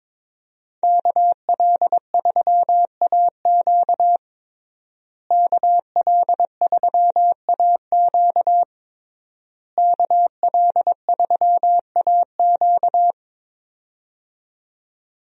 Morse code, heard as a keyed tone, about 22 wpm, 700 Hz. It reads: KL3AQ KL3AQ KL3AQ
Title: USA Call Signs - 5 Characters - Repeated 3x - 22wpm